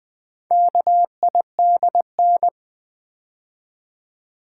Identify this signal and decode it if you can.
Morse code, heard as a keyed tone, about 20 wpm, 700 Hz. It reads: KIDN
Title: Build A Word - 2 Letters At A Time - 6-Letter Words - 20wpm